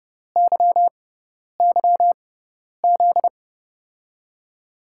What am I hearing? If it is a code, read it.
Morse code, heard as a keyed tone, about 32 wpm, spaced as for 10 wpm, 700 Hz. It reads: YYZ
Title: USA Call Signs - 6 Characters - 6x Character Spacing - 30wpm